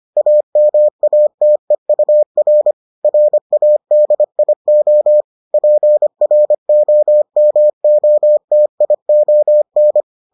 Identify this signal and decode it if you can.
Morse code, heard as a keyed tone, about 25 wpm, 600 Hz. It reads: AMATEUR RADIO PROMOTION